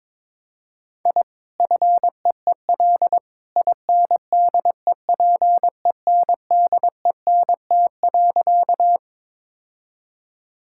Morse code, heard as a keyed tone, about 22 wpm, 700 Hz. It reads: I FEEL INDEPENDENT.
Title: Sentences from Top 1000 Words - Review Part 6 - 22wpm